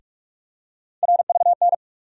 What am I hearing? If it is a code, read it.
Morse code, heard as a keyed tone, about 46 wpm, 700 Hz. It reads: RVN
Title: USA Call Sign Postfixes - 3 Letters 45wpm